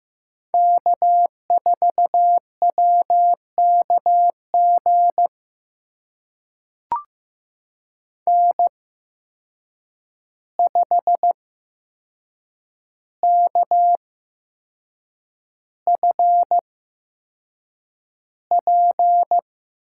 Morse code, heard as a keyed tone, about 15 wpm, spaced as for 6 wpm, 700 Hz. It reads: K4WKG N5KFP
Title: USA Call Signs - 5 Characters - 8x Character Spacing - 15wpm